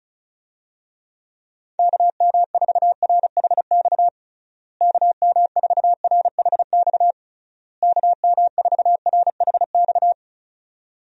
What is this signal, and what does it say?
Morse code, heard as a keyed tone, about 34 wpm, 700 Hz. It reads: KM4RHX KM4RHX KM4RHX
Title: USA Call Signs - 6 Characters - Repeated 3x - 35wpm